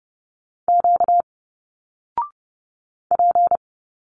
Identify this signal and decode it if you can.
Morse code, heard as a keyed tone, about 30 wpm, 700 Hz. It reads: Q P
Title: Single Letters 30wpm